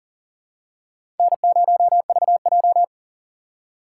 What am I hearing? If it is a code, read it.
Morse code, heard as a keyed tone, about 40 wpm, 690 Hz. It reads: N0VJ
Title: USA Call Signs - 4 Characters - 2x Character Spacing - 40wpm